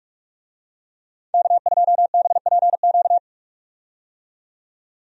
Morse code, heard as a keyed tone, about 47 wpm, 700 Hz. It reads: K2BPQ
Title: USA Call Signs - 5 Characters - 45wpm